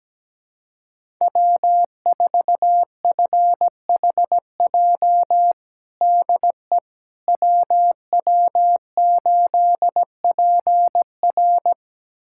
Morse code, heard as a keyed tone, about 17 wpm, 700 Hz. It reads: W4FHJ DE WW8PR
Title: USA Callsign DE Callsign - 5 Characters - 17wpm